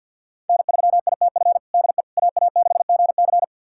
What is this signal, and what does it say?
Morse code, heard as a keyed tone, about 52 wpm, 700 Hz. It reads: N3ITF DE AA6GC
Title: USA Callsign DE Callsign - 5 Characters - 50wpm